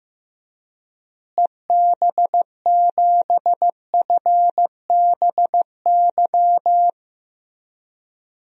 Morse code, heard as a keyed tone, about 15 wpm, 700 Hz. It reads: EB7FBY
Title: DXCC-100 Call Signs - 6 Characters - Repeated 3x - 15wpm